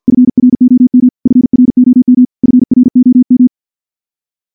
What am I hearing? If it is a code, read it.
Morse code, heard as a keyed tone, about 51 wpm, 250 Hz. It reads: FROM FROM FROM
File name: morse / morse-training-50WPM